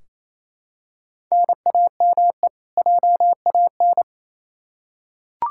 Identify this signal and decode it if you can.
Morse code, heard as a keyed tone, about 28 wpm, 700 Hz. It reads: NAME JAN
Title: Practice QSO - Exchange 1 - Beginning - 28wpm